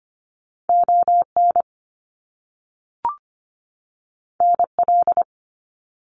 Morse code, heard as a keyed tone, about 25 wpm, 700 Hz. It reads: ON NL
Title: Canadian Provinces and Territories 25wpm